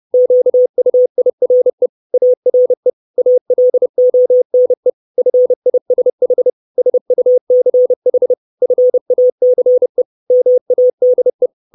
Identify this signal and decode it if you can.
Morse code, heard as a keyed tone, about 30 wpm, 500 Hz. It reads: QUIRE ARE ALONE FISH SUCH FACE MADE
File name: 632 Words 30WPM